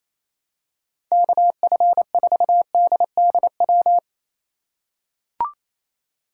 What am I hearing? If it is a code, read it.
Morse code, heard as a keyed tone, about 28 wpm, 700 Hz. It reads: KF4DDW
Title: USA Call Signs - 6 Characters - 4x Character Spacing - 28wpm